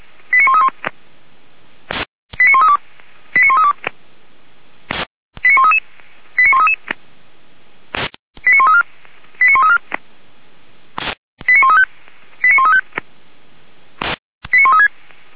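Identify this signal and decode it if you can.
Five-tone selective call (ZVEI), heard as quick runs of five tones, tones of about 70 ms. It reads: 89131 89132 89132 89133 89133 89134 89134 89135 89135 89136
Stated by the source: ZVEI five-tone selective call 89110, received on an SDR.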